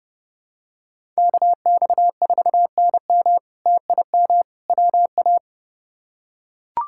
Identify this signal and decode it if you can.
Morse code, heard as a keyed tone, about 29 wpm, 700 Hz. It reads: KX4NM TIM WA
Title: K1USN SST QSO Practice - 30wpm